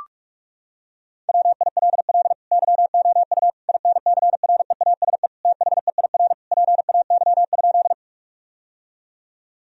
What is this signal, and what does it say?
Morse code, heard as a keyed tone, about 41 wpm, 700 Hz. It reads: WILL YOU INCREASE THEIR PAY?
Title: Sentences from Top 1000 Words - Review Part 4 - 45wpm